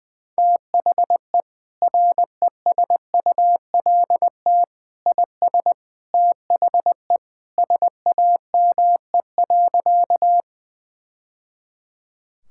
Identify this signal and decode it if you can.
Morse code, heard as a keyed tone, about 20 wpm, 700 Hz. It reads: THE RESULT IS THE SAME.
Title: Sentences from Top 1000 Words - Review Part 5 - 20wpm